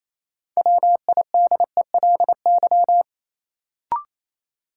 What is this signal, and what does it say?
Morse code, heard as a keyed tone, about 28 wpm, 700 Hz. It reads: WIDELY
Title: Sentences from Top 1200 Words - New Word Drill - 28wpm